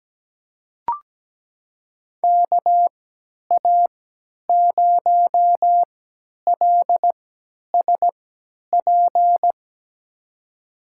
Morse code, heard as a keyed tone, about 17 wpm, spaced as for 10 wpm, 700 Hz. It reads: KA0LSP
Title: USA Call Signs - 6 Characters - 3x Character Spacing - 17wpm